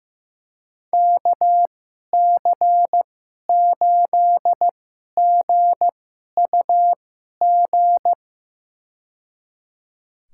Morse code, heard as a keyed tone, about 15 wpm, spaced as for 11 wpm, 700 Hz. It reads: KC8GUG
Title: USA Call Signs - 6 Characters - 2x Character Spacing - 15wpm